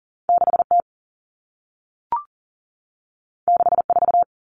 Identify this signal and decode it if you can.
Morse code, heard as a keyed tone, about 40 wpm, 700 Hz. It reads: 6T 64
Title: Common Cut Numbers - 2-Digit 40wpm